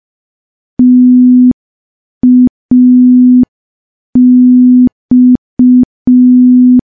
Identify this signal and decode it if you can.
Morse code, heard as a keyed tone, about 5 wpm, 250 Hz. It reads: TAX
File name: morse / morse-training-05WPM